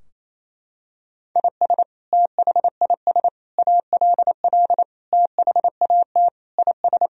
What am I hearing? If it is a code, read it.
Morse code, heard as a keyed tone, about 28 wpm, 700 Hz. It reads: IS THIS ALL THAT IS
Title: Sentences from Top 1000 Words - Review Part 3 - 28wpm